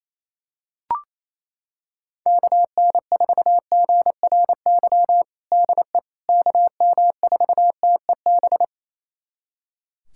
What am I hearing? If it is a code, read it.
Morse code, heard as a keyed tone, about 28 wpm, 700 Hz. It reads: KN4GRY DE KM4TEB
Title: USA Callsign DE Callsign 28wpm